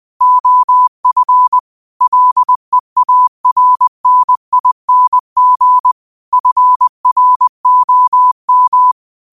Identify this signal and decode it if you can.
Morse code, heard as a keyed tone, about 20 wpm, 1000 Hz. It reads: OF LEARNING FROM
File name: qotdfiles / 20wpm